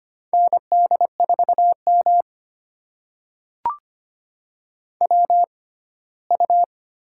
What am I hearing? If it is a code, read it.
Morse code, heard as a keyed tone, about 25 wpm, spaced as for 17 wpm, 700 Hz. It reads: ND4M WU
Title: USA Call Signs - 4 Characters - 6x Character Spacing - 25wpm